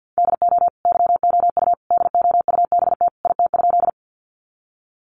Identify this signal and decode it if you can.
Morse code, heard as a keyed tone, about 50 wpm, 700 Hz. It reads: DO YOU DOUBT IT?
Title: Sentences from Top 1500 Words 50wpm